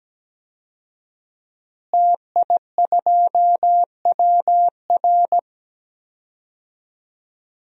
Morse code, heard as a keyed tone, about 17 wpm, 700 Hz. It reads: TI2WR
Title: DXCC-100 Call Signs - 5 Characters - Repeated 3x - 17wpm